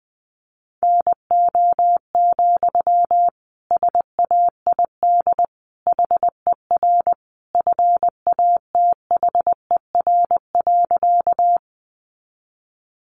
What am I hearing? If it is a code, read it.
Morse code, heard as a keyed tone, about 20 wpm, 700 Hz. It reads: NO, SAID HER FATHER.
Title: Sentences from Top 300 Words 20wpm